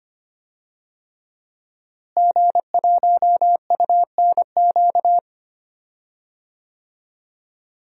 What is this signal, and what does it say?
Morse code, heard as a keyed tone, about 25 wpm, 700 Hz. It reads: G1UNQ
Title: DXCC-100 Call Signs - 5 Characters - Repeated 3x - 25wpm